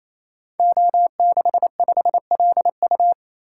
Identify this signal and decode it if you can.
Morse code, heard as a keyed tone, about 27 wpm, 700 Hz. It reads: O65LU
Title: Code Groups - 5 Letter:Numbers - 28wpm